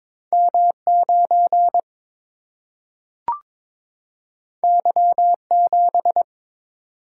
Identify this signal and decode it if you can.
Morse code, heard as a keyed tone, about 22 wpm, 700 Hz. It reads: M 9 Y 7
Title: Single-Letter Single-Number 22wpm